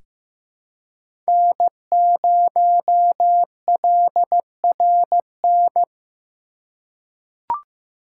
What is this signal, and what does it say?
Morse code, heard as a keyed tone, about 15 wpm, 700 Hz. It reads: N0LRN
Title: USA Call Signs - 5 Characters - 4x Character Spacing - 15wpm